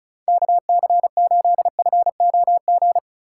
Morse code, heard as a keyed tone, about 34 wpm, 700 Hz. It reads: KC8FOG
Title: USA Call Signs - 6 Characters - 2x Character Spacing - 35wpm